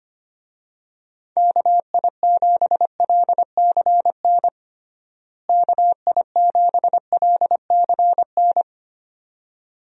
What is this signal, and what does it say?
Morse code, heard as a keyed tone, about 25 wpm, 700 Hz. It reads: KI7LCN KI7LCN
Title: USA Call Signs - 6 Characters - Repeated 2x - 25wpm